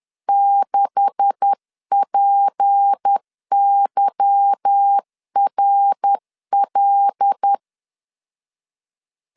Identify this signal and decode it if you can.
Morse code, heard as a keyed tone, about 10 wpm, 800 Hz. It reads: 6PYRL
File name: LEZIONI / 9 286 10wpm 2735